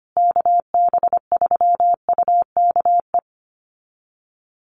Morse code, heard as a keyed tone, about 25 wpm, 700 Hz. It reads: KB3UKE
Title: USA Call Signs Repeated 2x 25wpm